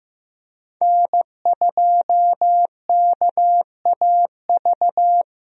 Morse code, heard as a keyed tone, about 15 wpm, 700 Hz. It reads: N2KAV
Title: USA Call Signs - 5 Characters - 15wpm